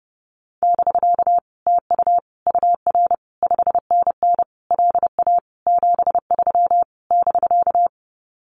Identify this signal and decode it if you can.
Morse code, heard as a keyed tone, about 30 wpm, 700 Hz. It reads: <BK> TU UR 5NN LA 73 <BK>
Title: Common QSO Phrases Encore 30wpm